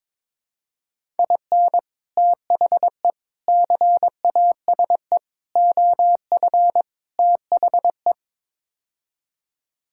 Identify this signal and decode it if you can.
Morse code, heard as a keyed tone, about 22 wpm, 700 Hz. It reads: IN THE CASE OF THE
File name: practice / 22wpm / Build A Phrase - 5-Word Phrases - 22wpm